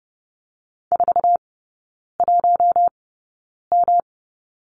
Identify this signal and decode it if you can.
Morse code, heard as a keyed tone, about 30 wpm, spaced as for 9 wpm, 700 Hz. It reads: 41M
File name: Lesson 19e - Letter w - Three Characters 30wpm